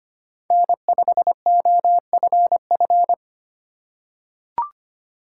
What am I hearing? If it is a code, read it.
Morse code, heard as a keyed tone, about 25 wpm, 700 Hz. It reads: N5OFF
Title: USA Call Signs - 5 Characters - 4x Character Spacing - 25wpm